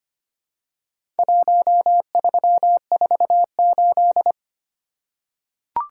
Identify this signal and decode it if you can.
Morse code, heard as a keyed tone, about 25 wpm, 700 Hz. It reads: 1348
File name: practice / 25wpm / Four Letter-Number Combinations - 25wpm